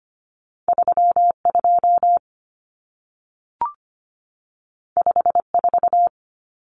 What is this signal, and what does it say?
Morse code, heard as a keyed tone, about 25 wpm, 700 Hz. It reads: 32 54
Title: Numbers - 2-Digit 25wpm